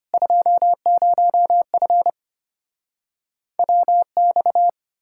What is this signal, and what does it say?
Morse code, heard as a keyed tone, about 33 wpm, 700 Hz. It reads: 20F WX